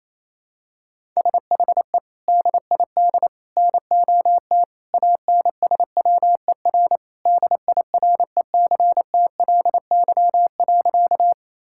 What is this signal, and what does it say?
Morse code, heard as a keyed tone, about 28 wpm, 700 Hz. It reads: SHE DID NOT ANSWER DIRECTLY.
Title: Sentences from Top 1000 Words - Review Part 5 - 28wpm